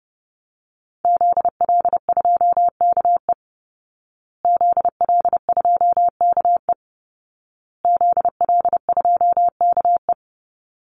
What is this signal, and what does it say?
Morse code, heard as a keyed tone, about 30 wpm, 700 Hz. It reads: ZL2KE ZL2KE ZL2KE
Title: Call Signs Repeated 2x - 30wpm